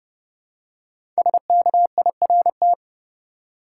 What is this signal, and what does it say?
Morse code, heard as a keyed tone, about 31 wpm, 700 Hz. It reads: SKIRT
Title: Five Letter Words - 30wpm